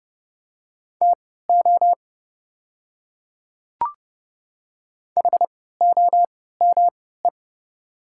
Morse code, heard as a keyed tone, about 29 wpm, spaced as for 17 wpm, 700 Hz. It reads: TO HOME
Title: 3x Character Spacing - Top 500 Words - 30wpm